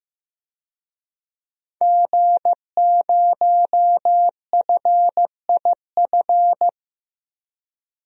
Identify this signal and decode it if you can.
Morse code, heard as a keyed tone, about 15 wpm, 700 Hz. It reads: G0FIF
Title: DXCC-100 Call Signs - 5 Characters - Repeated 3x - 15wpm